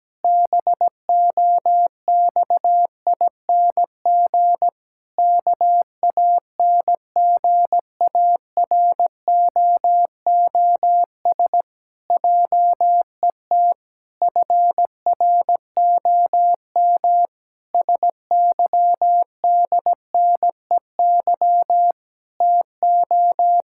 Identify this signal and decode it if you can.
Morse code, heard as a keyed tone, about 17 wpm, 700 Hz. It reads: BOXING KANGAROOS JET FROM SYDNEY TO
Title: Pangram Challenge v2 - 17wpm